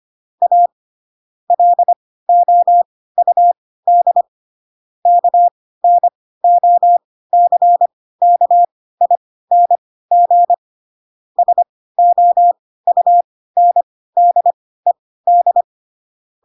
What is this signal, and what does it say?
Morse code, heard as a keyed tone, about 25 wpm, spaced as for 16 wpm, 700 Hz. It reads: A LOUD KNOCKING SOUNDED